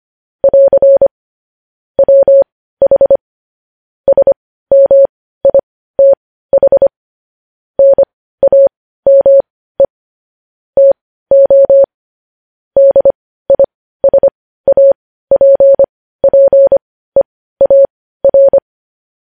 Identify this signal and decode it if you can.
Morse code, heard as a keyed tone, about 25 wpm, spaced as for 15 wpm, 550 Hz. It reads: <AR> WH SMITH NAME TO DISAPPEAR
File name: podcast / News headlines-1x-25x15